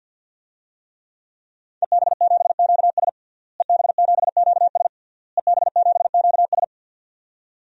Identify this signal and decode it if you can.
Morse code, heard as a keyed tone, about 45 wpm, 700 Hz. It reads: EB7QS EB7QS EB7QS
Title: DXCC-100 Call Signs - 5 Characters - Repeated 3x - 50wpm